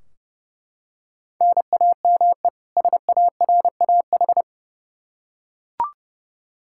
Morse code, heard as a keyed tone, about 30 wpm, 700 Hz. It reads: NAME SARAH
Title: K1USN SST QSO Practice 30wpm